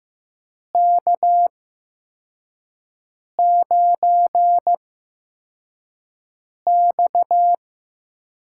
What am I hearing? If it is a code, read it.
Morse code, heard as a keyed tone, about 15 wpm, 700 Hz. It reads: K 9 X
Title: USA Call Signs - 8x Character Spacing - 15wpm